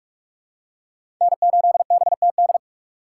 Morse code, heard as a keyed tone, about 41 wpm, 700 Hz. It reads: N8BTD
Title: USA Call Signs - 2x Character Spacing - 45wpm